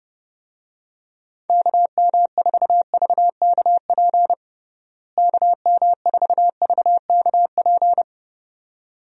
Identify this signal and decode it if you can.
Morse code, heard as a keyed tone, about 32 wpm, 700 Hz. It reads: KM4VKP KM4VKP
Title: USA Call Signs - 6 Characters - Repeated 2x - 30wpm